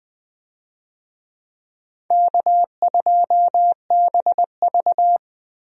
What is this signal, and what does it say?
Morse code, heard as a keyed tone, about 20 wpm, 700 Hz. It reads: K2BV
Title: USA Call Signs - 4 Characters - Repeated 2x - 20wpm